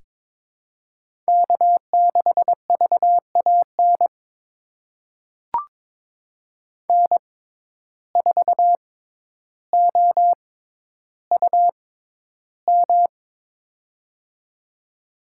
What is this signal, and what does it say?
Morse code, heard as a keyed tone, about 22 wpm, spaced as for 11 wpm, 700 Hz. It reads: K6VAN N4OUM
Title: USA Call Signs - 5 Characters - 6x Character Spacing - 22wpm